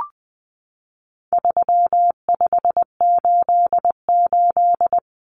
Morse code, heard as a keyed tone, about 20 wpm, 700 Hz. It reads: 3588
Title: Common Cut Numbers - 4-Digit 20wpm